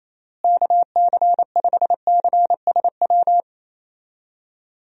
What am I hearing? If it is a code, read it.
Morse code, heard as a keyed tone, about 28 wpm, 700 Hz. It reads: KC5CSW
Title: USA Call Signs - 6 Characters - Repeated 3x - 28wpm